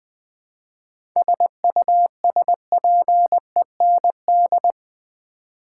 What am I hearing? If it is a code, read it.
Morse code, heard as a keyed tone, about 21 wpm, 700 Hz. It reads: SUSPEND
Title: 7 Letter Words - First 3 Letters Given - 20wpm